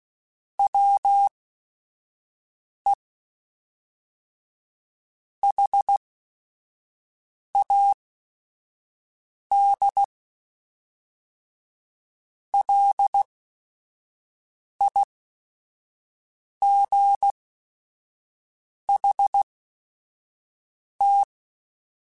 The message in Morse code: WE HAD LIGHT

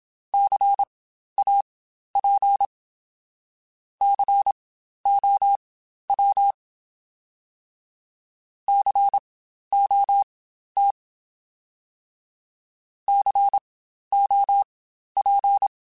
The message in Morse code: CAP COW COT COP

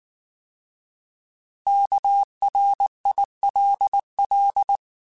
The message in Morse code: KRILL

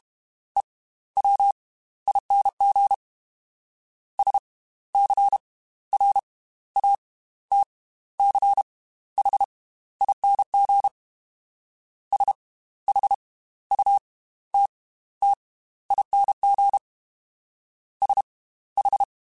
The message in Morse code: EWING SCRATCHING SHUTTING SH

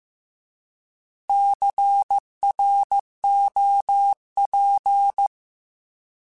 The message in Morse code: CROP

